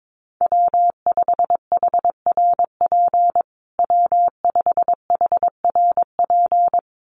Morse code, heard as a keyed tone, about 22 wpm, 700 Hz. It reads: W5HRP W5HRP
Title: USA Call Signs Repeated 2x 22wpm